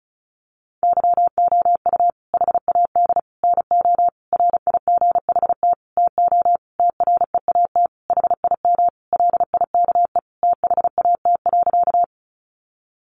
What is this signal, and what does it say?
Morse code, heard as a keyed tone, about 34 wpm, 700 Hz. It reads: YOU HAD NO RIGHT TO TREAT HIM LIKE THAT.
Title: Sentences from Top 1300 Words 35wpm